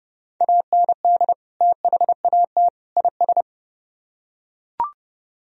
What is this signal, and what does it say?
Morse code, heard as a keyed tone, about 31 wpm, 700 Hz. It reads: AND THAT IS